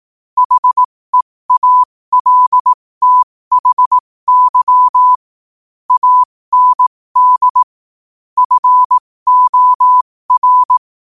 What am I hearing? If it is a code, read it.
Morse code, heard as a keyed tone, about 18 wpm, spaced as for 15 wpm, 1000 Hz. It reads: HEALTHY AND FOR
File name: qotdfiles / 15wpm